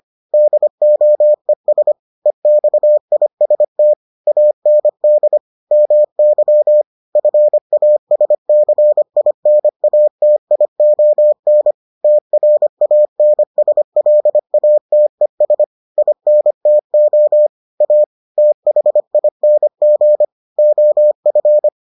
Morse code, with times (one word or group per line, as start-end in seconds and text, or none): 0.3-1.9 DOES
2.3-3.9 EXIST
4.3-5.4 AND
5.7-6.8 MY
7.2-11.7 FASCINATION
12.0-15.6 TRANSLATES
16.0-17.5 INTO
17.8-18.0 A
18.4-20.2 THING
20.6-21.7 OF